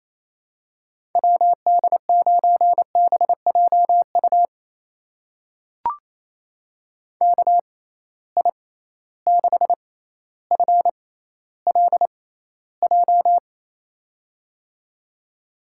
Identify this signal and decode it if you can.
Morse code, heard as a keyed tone, about 28 wpm, spaced as for 14 wpm, 700 Hz. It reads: WD9BJU KI6FLJ
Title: USA Call Signs - 6 Characters - 6x Character Spacing - 28wpm